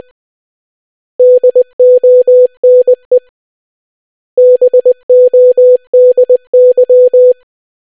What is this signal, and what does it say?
Morse code, heard as a keyed tone, about 20 wpm, 500 Hz. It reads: DONE BODY